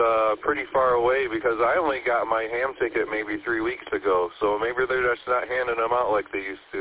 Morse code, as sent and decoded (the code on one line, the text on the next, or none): none